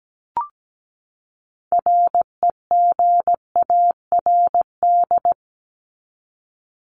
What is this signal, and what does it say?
Morse code, heard as a keyed tone, about 17 wpm, 700 Hz. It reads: REGARD
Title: ED Suffix - Top 100 Words with Base Words 17wpm